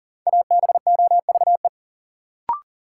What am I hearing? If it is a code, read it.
Morse code, heard as a keyed tone, about 40 wpm, 700 Hz. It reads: ABOVE
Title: Sentences from Top 1000 Words - Review Part 2 - New Word Drill - 40wpm